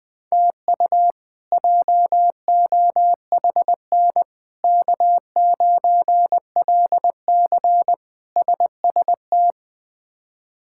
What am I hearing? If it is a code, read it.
Morse code, heard as a keyed tone, about 21 wpm, 700 Hz. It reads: TU JOHN K9LC SST
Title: K1USN SST QSO Practice - 20wpm